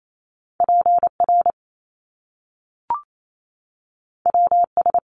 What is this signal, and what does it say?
Morse code, heard as a keyed tone, about 28 wpm, 700 Hz. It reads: PR WS
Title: Top Two Letter Combinations in Words 28wpm